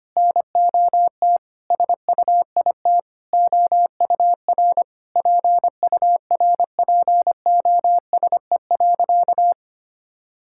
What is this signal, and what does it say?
Morse code, heard as a keyed tone, about 25 wpm, 700 Hz. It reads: NOT SUIT OUR PURPOSE.
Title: Sentences from Top 1600 Words 25wpm